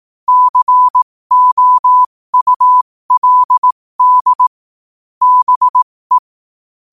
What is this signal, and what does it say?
Morse code, heard as a keyed tone, about 18 wpm, spaced as for 15 wpm, 1000 Hz. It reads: COULD BE
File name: qotdfiles / 15wpm